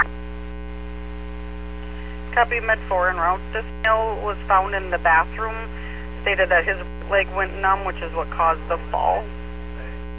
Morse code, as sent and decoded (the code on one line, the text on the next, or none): none